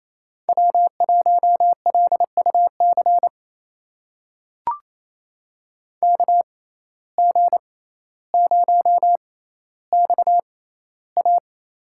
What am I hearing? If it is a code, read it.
Morse code, heard as a keyed tone, about 28 wpm, spaced as for 14 wpm, 700 Hz. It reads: W1LUC KG0XA
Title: USA Call Signs - 5 Characters - 6x Character Spacing - 28wpm